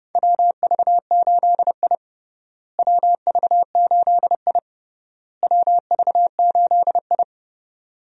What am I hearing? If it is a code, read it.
Morse code, heard as a keyed tone, about 32 wpm, 700 Hz. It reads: WV8I WV8I WV8I
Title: USA Call Signs - 4 Characters - Repeated 3x - 30wpm